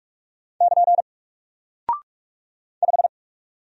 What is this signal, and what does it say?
Morse code, heard as a keyed tone, about 46 wpm, 700 Hz. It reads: <KN> 5
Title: Single Character (Letter:Number:Punctuation:Prosign) 45wpm